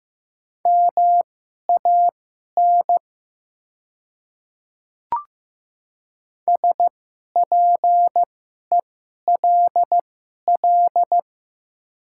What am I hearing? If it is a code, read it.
Morse code, heard as a keyed tone, about 15 wpm, spaced as for 11 wpm, 700 Hz. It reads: MAN SPELL
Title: 2x Character Spacing - Top 300 Words - 15wpm